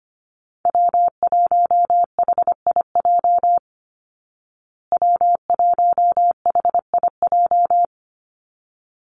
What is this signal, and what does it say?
Morse code, heard as a keyed tone, about 25 wpm, 700 Hz. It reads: W1HIJ W1HIJ
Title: Call Signs Repeated 3x - 25wpm